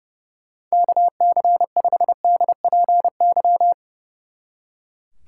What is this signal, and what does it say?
Morse code, heard as a keyed tone, about 36 wpm, 700 Hz. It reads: KC5DPY
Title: Speed Racing - USA 2x3 Call Signs 30wpm